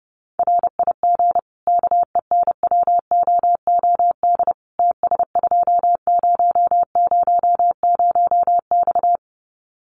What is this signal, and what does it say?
Morse code, heard as a keyed tone, about 30 wpm, 700 Hz. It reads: RIG KENWOOD TS2000X
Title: Common QSO Phrases 30wpm